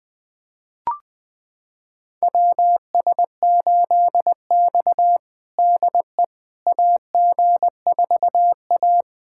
Morse code, heard as a keyed tone, about 20 wpm, 700 Hz. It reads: WS8X DE AG4A
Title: USA Callsign DE Callsign - 4 Characters - 20wpm